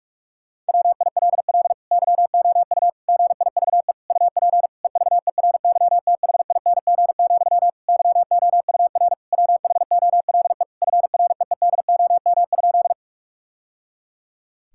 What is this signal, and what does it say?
Morse code, heard as a keyed tone, about 46 wpm, 700 Hz. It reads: WILL YOU GIVE UP EVERYTHING, YOUR WHOLE FREEDOM?
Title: Sentences from Top 1000 Words - Review Part 5 - 45wpm